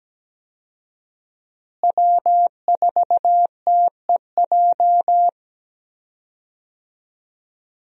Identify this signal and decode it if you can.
Morse code, heard as a keyed tone, about 17 wpm, 700 Hz. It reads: W4TEJ
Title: USA Call Signs - 5 Characters - Repeated 3x - 17wpm